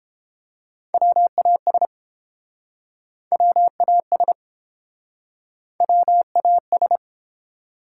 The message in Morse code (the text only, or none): WAS WAS WAS